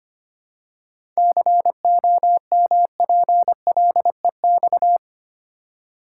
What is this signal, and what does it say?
Morse code, heard as a keyed tone, about 25 wpm, 700 Hz. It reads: COMPLEX
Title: Seven Letter Words - 25wpm